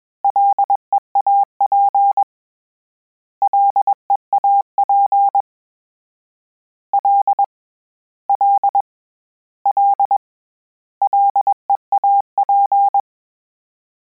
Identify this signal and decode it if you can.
Morse code, heard as a keyed tone, about 21 wpm, 800 Hz. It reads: LEAP LEAP L L L LEAP
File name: Combinations / Combinations-mp3s / Combinations@20